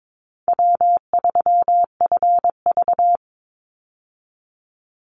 Morse code, heard as a keyed tone, about 22 wpm, 700 Hz. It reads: W3FV
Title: Easy Call Signs - 22wpm